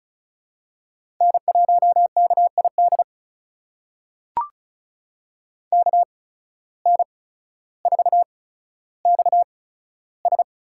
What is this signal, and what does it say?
Morse code, heard as a keyed tone, about 36 wpm, spaced as for 15 wpm, 700 Hz. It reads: N1KID KN4XS
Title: USA Call Signs - 5 Characters - 8x Character Spacing - 35wpm